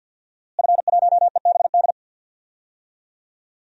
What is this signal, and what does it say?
Morse code, heard as a keyed tone, about 50 wpm, 700 Hz. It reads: F1EBD
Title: DXCC-100 Call Signs - 5 Characters - Repeated 3x - 50wpm